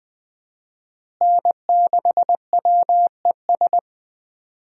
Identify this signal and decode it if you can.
Morse code, heard as a keyed tone, about 20 wpm, 700 Hz. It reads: N6WES
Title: USA Call Signs - 5 Characters - Repeated 3x - 20wpm